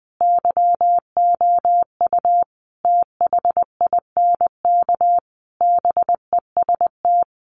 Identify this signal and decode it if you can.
Morse code, heard as a keyed tone, about 20 wpm, 700 Hz. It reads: YOU THINK BEST
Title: Sentences from Top 1100 Words 20wpm